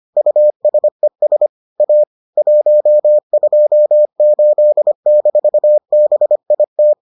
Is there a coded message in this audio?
Yes: Morse code from 0.2 to 6.9 s, heard as a keyed tone, about 25 wpm, 600 Hz. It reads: USES A 128-BIT